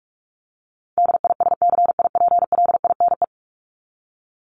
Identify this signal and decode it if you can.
Morse code, heard as a keyed tone, about 44 wpm, 700 Hz. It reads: DISCIPLINE
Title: Top 751-1000 Words 45wpm